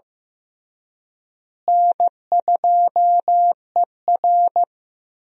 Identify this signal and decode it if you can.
Morse code, heard as a keyed tone, about 15 wpm, 700 Hz. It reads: N2ER
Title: USA Call Signs - 4 Characters - Repeated 3x - 15wpm